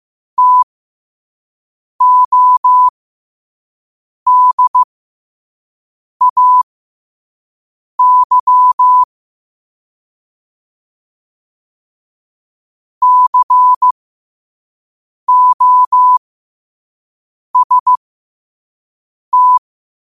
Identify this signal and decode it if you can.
Morse code, heard as a keyed tone, about 15 wpm, spaced as for 5 wpm, 1000 Hz. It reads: TODAY COST
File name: qotdfiles / 5wpm